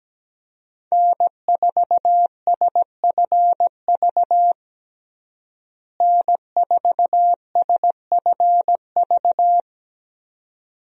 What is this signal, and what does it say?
Morse code, heard as a keyed tone, about 17 wpm, 700 Hz. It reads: N4SFV N4SFV
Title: USA Call Signs - 5 Characters - Repeated 2x - 17wpm